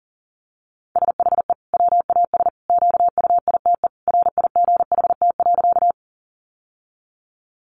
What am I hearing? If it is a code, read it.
Morse code, heard as a keyed tone, about 40 wpm, 700 Hz. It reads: SHE WAS QUITE RIGHT.
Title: Sentences from Top 700 Words 40wpm